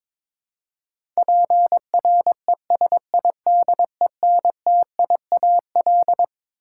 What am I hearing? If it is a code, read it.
Morse code, heard as a keyed tone, about 22 wpm, 700 Hz. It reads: PRESIDENTIAL
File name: Twelve Letter Words - 22wpm